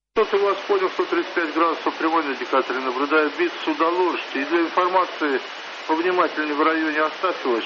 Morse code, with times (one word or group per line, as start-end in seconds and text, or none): none